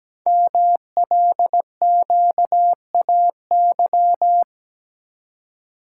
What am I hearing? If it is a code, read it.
Morse code, heard as a keyed tone, about 17 wpm, 700 Hz. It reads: MLQAY
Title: Code Groups - 5 Letters - 17wpm